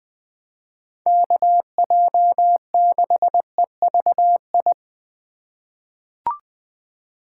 Morse code, heard as a keyed tone, about 20 wpm, 700 Hz. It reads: KJ6EVI
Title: USA Call Signs - 6 Characters - 4x Character Spacing - 20wpm